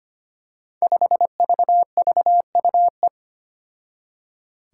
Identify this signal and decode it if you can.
Morse code, heard as a keyed tone, about 25 wpm, 700 Hz. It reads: 5VVUE
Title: Five Challenging Letter-Number Combinations 25wpm